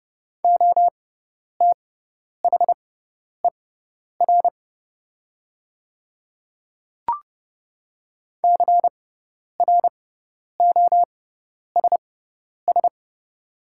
Morse code, heard as a keyed tone, about 30 wpm, spaced as for 10 wpm, 700 Hz. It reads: OTHER CROSS
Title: 6x Character Spacing - Top 500 Words - 30wpm